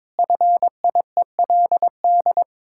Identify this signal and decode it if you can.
Morse code, heard as a keyed tone, about 22 wpm, 700 Hz. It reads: FIELD
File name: practice / 22wpm / Sets of 6 Words - 22wpm